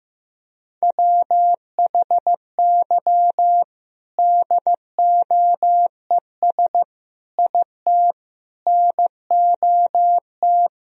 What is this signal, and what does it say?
Morse code, heard as a keyed tone, about 15 wpm, 700 Hz. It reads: WHY DOES IT NOT
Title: Sentences from Top 1000 Words - Review Part 3 - 15wpm